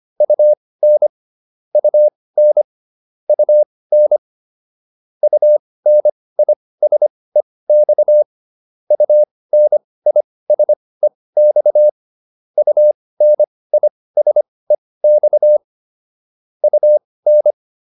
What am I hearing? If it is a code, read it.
Morse code, heard as a keyed tone, about 25 wpm, spaced as for 18 wpm, 600 Hz. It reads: UN UN UN UNISEX UNISEX UNISEX UN